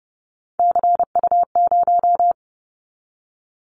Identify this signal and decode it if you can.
Morse code, heard as a keyed tone, about 30 wpm, 700 Hz. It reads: CU0
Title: Callsign Prefixes for Top 100 DXCC - 30wpm